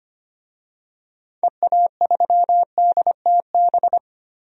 Morse code, heard as a keyed tone, about 25 wpm, 700 Hz. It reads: EA3DTB
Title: DXCC-100 Call Signs - 6 Characters - Repeated 2x - 25wpm